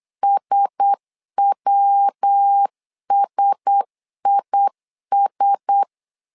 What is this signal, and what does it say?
Morse code, heard as a keyed tone, about 8 wpm, 800 Hz. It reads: SWSIS
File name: LEZIONI / 4 Q1S 8wpm 2710